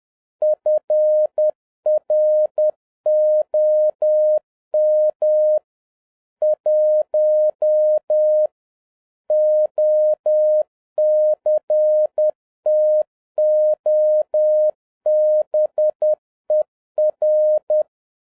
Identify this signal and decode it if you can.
Morse code, heard as a keyed tone, about 10 wpm, 610 Hz. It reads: FROM 1 OCTOBER